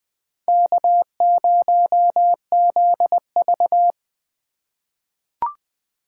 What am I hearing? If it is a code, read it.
Morse code, heard as a keyed tone, about 20 wpm, 700 Hz. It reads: K0ZV
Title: USA Call Signs - 4 Characters - 8x Character Spacing - 20wpm